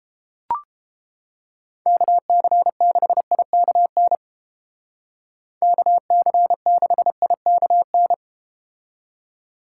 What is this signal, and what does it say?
Morse code, heard as a keyed tone, about 32 wpm, 700 Hz. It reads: KC6IKN KC6IKN